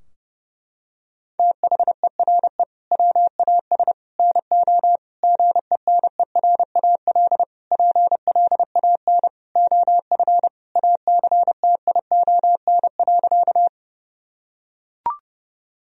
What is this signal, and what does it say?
Morse code, heard as a keyed tone, about 32 wpm, 700 Hz. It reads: THERE WAS NO GENERAL PLAN OF ACTION.
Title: Sentences from Top 1000 Words - Review Part 4 - 30wpm